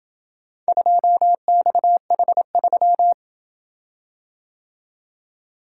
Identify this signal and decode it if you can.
Morse code, heard as a keyed tone, about 27 wpm, 700 Hz. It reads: 2XH3